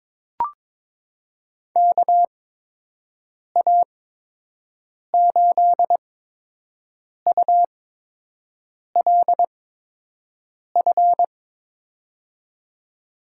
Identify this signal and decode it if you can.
Morse code, heard as a keyed tone, about 22 wpm, spaced as for 6 wpm, 700 Hz. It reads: KA8ULF
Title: USA Call Signs - 6 Characters - 8x Character Spacing - 22wpm